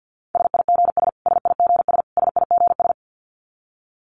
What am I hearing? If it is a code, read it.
Morse code, heard as a keyed tone, about 50 wpm, 700 Hz. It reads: SIGS SIGS SIGS
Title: Abbreviations Repeated 3x - 50wpm